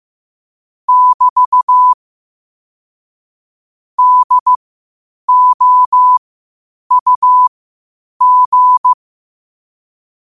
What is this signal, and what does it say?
Morse code, heard as a keyed tone, about 15 wpm, spaced as for 8 wpm, 1000 Hz. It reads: <BT> DOUG
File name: qotdfiles / 8wpm